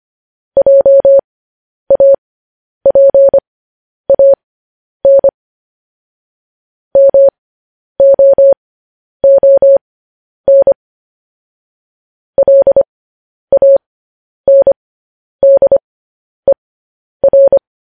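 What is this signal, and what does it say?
Morse code, heard as a keyed tone, about 25 wpm, spaced as for 10 wpm, 550 Hz. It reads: JAPAN MOON LANDER
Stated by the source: Morse code news headlines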